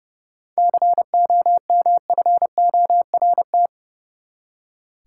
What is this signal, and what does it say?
Morse code, heard as a keyed tone, about 30 wpm, 700 Hz. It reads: COMFORT
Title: Seven Letter Words - 30wpm